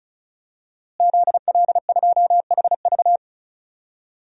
Morse code, heard as a keyed tone, about 35 wpm, 700 Hz. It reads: ZL2HV
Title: DXCC-100 Call Signs - 5 Characters - Repeated 3x - 35wpm